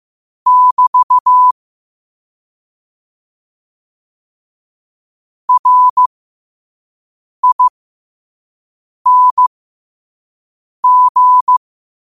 Morse code, heard as a keyed tone, about 15 wpm, spaced as for 5 wpm, 1000 Hz. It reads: <BT> RING
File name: qotdfiles / 5wpm